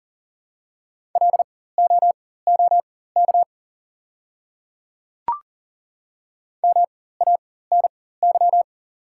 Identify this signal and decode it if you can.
Morse code, heard as a keyed tone, about 40 wpm, spaced as for 19 wpm, 700 Hz. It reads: LOOK MANY
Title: 4x Character Spacing - Top 100 Words - 40wpm